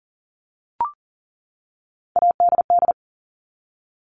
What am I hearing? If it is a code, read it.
Morse code, heard as a keyed tone, about 40 wpm, 700 Hz. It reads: ADD
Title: ED Suffix - Top 100 Words with Base Words 40wpm